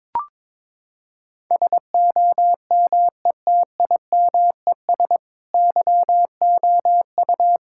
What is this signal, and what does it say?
Morse code, heard as a keyed tone, about 22 wpm, 700 Hz. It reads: SOMETIMES YOU
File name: Sentences with 7-Word Phrases - 22wpm